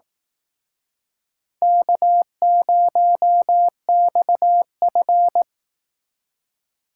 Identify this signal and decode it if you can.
Morse code, heard as a keyed tone, about 19 wpm, 700 Hz. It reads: K0XF